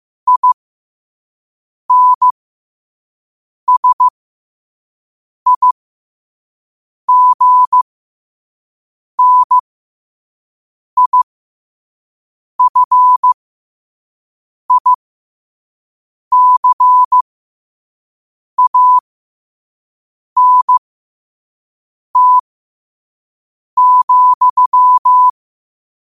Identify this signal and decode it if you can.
Morse code, heard as a keyed tone, about 15 wpm, spaced as for 5 wpm, 1000 Hz. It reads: INSIGNIFICANT,
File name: qotdfiles / 5wpm